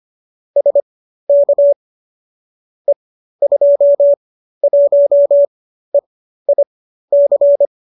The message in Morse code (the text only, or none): SK E21EIC